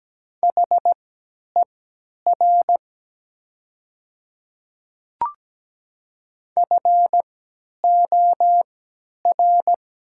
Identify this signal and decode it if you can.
Morse code, heard as a keyed tone, about 17 wpm, spaced as for 10 wpm, 700 Hz. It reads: HER FOR
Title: 3x Character Spacing - Top 100 Words - 17wpm